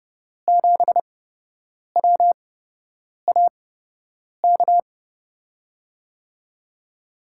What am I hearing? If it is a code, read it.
Morse code, heard as a keyed tone, about 32 wpm, spaced as for 8 wpm, 700 Hz. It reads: 7WAK
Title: USA Call Signs - 5 Characters - 8x Character Spacing - 30wpm